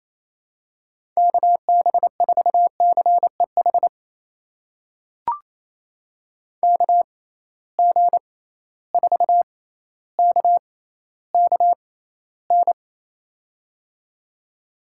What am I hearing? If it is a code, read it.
Morse code, heard as a keyed tone, about 28 wpm, spaced as for 14 wpm, 700 Hz. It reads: KB4CEH KG4KKN
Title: USA Call Signs - 6 Characters - 6x Character Spacing - 28wpm